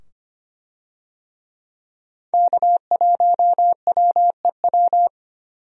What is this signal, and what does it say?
Morse code, heard as a keyed tone, about 25 wpm, 700 Hz. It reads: K1WEW
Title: USA Call Signs - 5 Characters - 25wpm